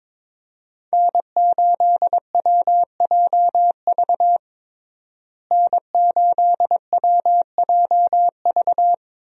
Morse code, heard as a keyed tone, about 22 wpm, 700 Hz. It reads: N8WJV N8WJV
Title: USA Call Signs - 5 Characters - Repeated 3x - 22wpm